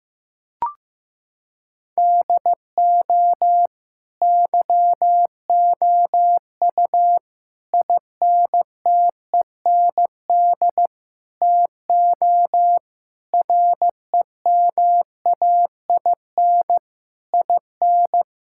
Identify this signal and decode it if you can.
Morse code, heard as a keyed tone, about 16 wpm, 700 Hz. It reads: DO YOU INTEND TO REMAIN IN
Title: Sentences from Top 1600 Words 15wpm